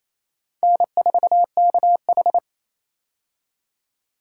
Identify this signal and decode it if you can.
Morse code, heard as a keyed tone, about 28 wpm, 700 Hz. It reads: N4KH
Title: K1USN SST QSO Practice - 28wpm